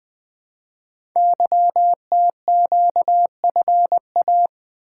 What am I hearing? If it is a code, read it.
Morse code, heard as a keyed tone, about 20 wpm, 700 Hz. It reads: YTQFA
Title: Code Groups - 5 Letters - 20wpm